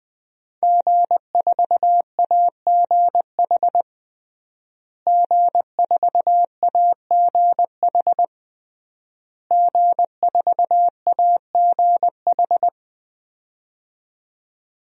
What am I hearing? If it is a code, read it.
Morse code, heard as a keyed tone, about 21 wpm, 700 Hz. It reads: G4AGH G4AGH G4AGH
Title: DXCC-100 Call Signs - 5 Characters - Repeated 3x - 20wpm